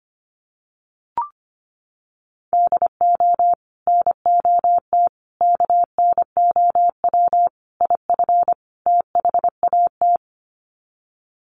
Morse code, heard as a keyed tone, about 25 wpm, 700 Hz. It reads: DO NOT KNOW IF THAT
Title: Sets of 5 Words Encore2 25wpm